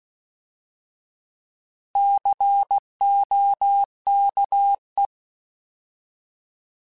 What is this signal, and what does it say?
Morse code, heard as a keyed tone, about 16 wpm, 800 Hz. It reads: COKE
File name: Alphabetical / Alphabetical mp3s / Cs @15